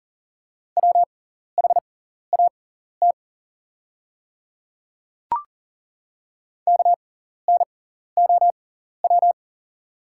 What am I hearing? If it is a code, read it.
Morse code, heard as a keyed tone, about 40 wpm, spaced as for 14 wpm, 700 Hz. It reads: WHAT KNOW
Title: 6x Character Spacing - Top 100 Words - 40wpm